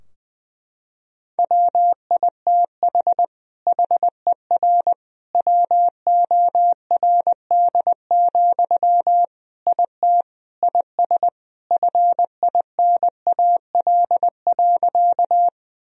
Morse code, heard as a keyed tone, about 21 wpm, 700 Hz. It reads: WITH HER WORD, IT IS FINAL.
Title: Sentences from Top 1000 Words - Review Part 2 - 20wpm